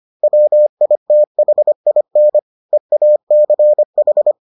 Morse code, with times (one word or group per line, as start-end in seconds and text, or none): 0.2-2.4 WITHIN
2.7-4.3 EACH